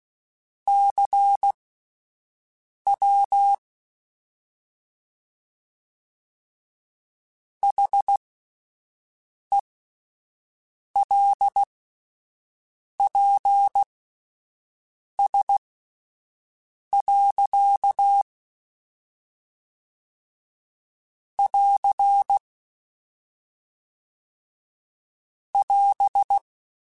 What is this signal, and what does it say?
Morse code, heard as a keyed tone, about 16 wpm, spaced as for 5 wpm, 800 Hz. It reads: CW HELPS. <AR> <AS>
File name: Misc / Misc-mp3s / QST QST @10